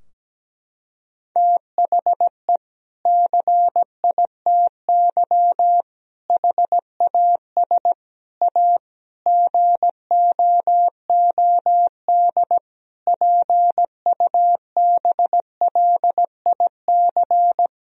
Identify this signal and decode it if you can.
Morse code, heard as a keyed tone, about 17 wpm, 700 Hz. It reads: THE CITY HAS A GOOD PUBLIC
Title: Sentences from Top 1000 Words - Review Part 4 - 17wpm